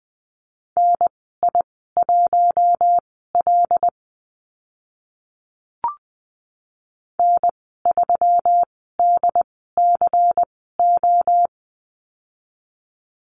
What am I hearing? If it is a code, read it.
Morse code, heard as a keyed tone, about 20 wpm, spaced as for 15 wpm, 700 Hz. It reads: NI1L N3DCO
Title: 122 Lesson 20g - Number 3 - Call Signs 20wpm